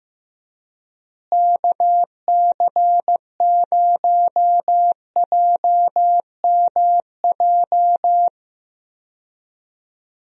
Morse code, heard as a keyed tone, about 15 wpm, 700 Hz. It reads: KC0JMJ
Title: USA Call Signs - 6 Characters - Repeated 3x - 15wpm